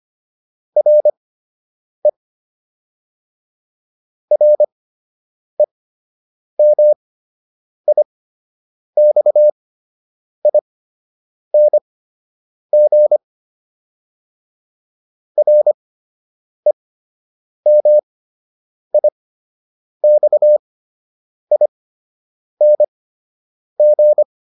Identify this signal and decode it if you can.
Morse code, heard as a keyed tone, about 25 wpm, spaced as for 8 wpm, 600 Hz. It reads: RE REMIXING REMIXING